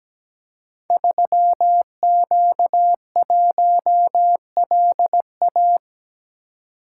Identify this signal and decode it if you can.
Morse code, heard as a keyed tone, about 17 wpm, 700 Hz. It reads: 3Q1LA
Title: Code Groups - 5 Letter:Numbers - 17wpm